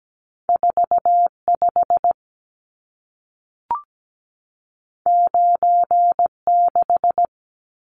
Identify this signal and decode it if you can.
Morse code, heard as a keyed tone, about 17 wpm, 700 Hz. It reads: 45 96
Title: Numbers - 2-Digit 17wpm